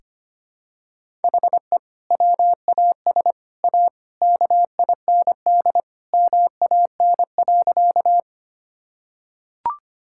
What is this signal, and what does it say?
Morse code, heard as a keyed tone, about 25 wpm, 700 Hz. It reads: HE WAS A KIND MAN.
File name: practice / 25wpm / Sentences with 3-Word Phrases - 25wpm